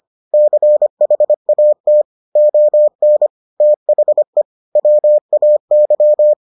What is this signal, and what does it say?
Morse code, heard as a keyed tone, about 25 wpm, 600 Hz. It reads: CHAT ON THE WAY